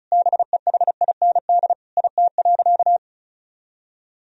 Morse code, heard as a keyed tone, about 36 wpm, 700 Hz. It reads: BEHIND IT.
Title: Sentences from Top 500 Words and Abbreviations 35wpm